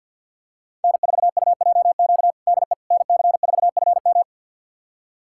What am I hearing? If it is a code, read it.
Morse code, heard as a keyed tone, about 51 wpm, 700 Hz. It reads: N4UJQ DE NC4FM
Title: USA Callsign DE Callsign - 5 Characters - 50wpm